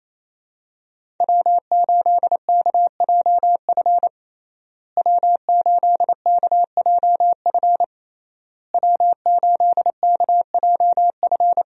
Morse code, heard as a keyed tone, about 28 wpm, 700 Hz. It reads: W8KJF W8KJF W8KJF
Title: USA Call Signs - 5 Characters - Repeated 3x - 28wpm